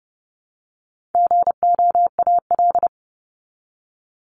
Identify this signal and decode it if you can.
Morse code, heard as a keyed tone, about 29 wpm, 700 Hz. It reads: GOAL
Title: Top 751-1000 Words 30wpm